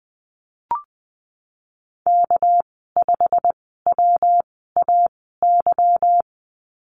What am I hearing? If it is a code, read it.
Morse code, heard as a keyed tone, about 20 wpm, spaced as for 15 wpm, 700 Hz. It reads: K5WAY
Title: 213 Lesson 34g - Number 8 - Call Signs 20wpm